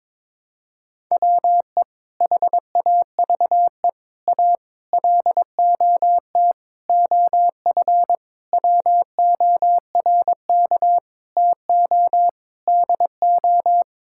Morse code, heard as a keyed tone, about 22 wpm, 700 Hz. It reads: WE HAVE A LOT OF WORK TO DO
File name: Sentences with 7-Word Phrases - 22wpm